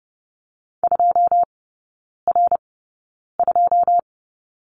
Morse code, heard as a keyed tone, about 30 wpm, spaced as for 9 wpm, 700 Hz. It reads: 2R2
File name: 081 Lesson 14e - Number 2 - Three Characters 30wpm